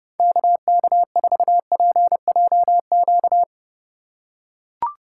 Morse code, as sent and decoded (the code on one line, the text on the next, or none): -.- -.- ....- .--. .--- --.-
KK4PJQ